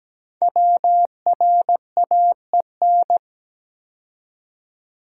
Morse code, heard as a keyed tone, about 17 wpm, 700 Hz. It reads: WRAEN
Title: Code Groups - 5 Letters - 17wpm